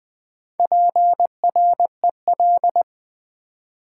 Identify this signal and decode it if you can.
Morse code, heard as a keyed tone, about 20 wpm, 700 Hz. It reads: PREL
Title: Build A Word - 4 Letters At A Time - 11-Letter Words - 20wpm